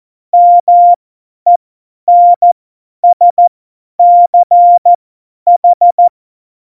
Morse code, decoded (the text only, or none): MENSCH